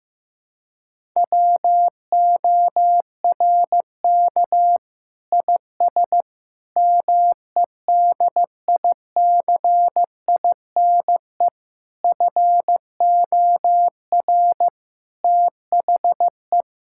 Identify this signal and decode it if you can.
Morse code, heard as a keyed tone, about 15 wpm, 700 Hz. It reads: WORK IS MEDICINE FOR THE
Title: Sentences from Top 1000 Words - Review Part 6 - 15wpm